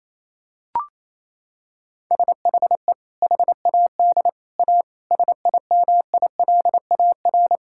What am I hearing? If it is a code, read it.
Morse code, heard as a keyed tone, about 28 wpm, 700 Hz. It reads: SHE HAD A SIMILAR
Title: Sentences from Top 1000 Words - Review Part 5 - 28wpm